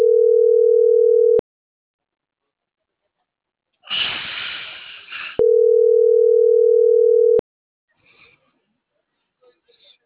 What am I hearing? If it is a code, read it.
Ringback tone (North America), 440 + 480 Hz.